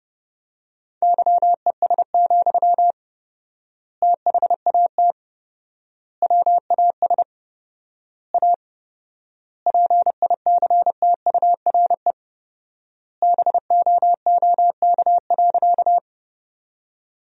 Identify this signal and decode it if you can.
Morse code, heard as a keyed tone, about 30 wpm, 700 Hz. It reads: YES, THAT WAS A PICTURE BOOK.
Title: 4x Word-Spacing - Sentences from Top 500 Words - 30wpm